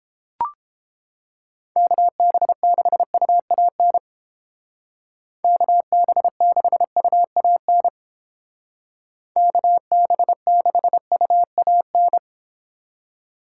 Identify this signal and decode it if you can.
Morse code, heard as a keyed tone, about 29 wpm, 700 Hz. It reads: KB6UAN KB6UAN KB6UAN